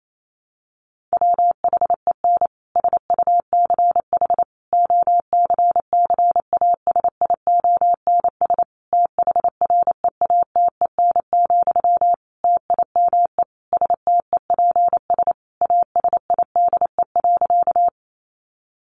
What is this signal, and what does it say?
Morse code, heard as a keyed tone, about 28 wpm, 700 Hz. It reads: WHEN SUCH OCCASIONS THREATEN, TIME STEPS ASIDE.
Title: A Sound Of Thunder 28wpm